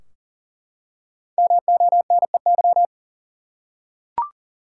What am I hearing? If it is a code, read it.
Morse code, heard as a keyed tone, about 37 wpm, 700 Hz. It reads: MONEY
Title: Five Letter Words - 40wpm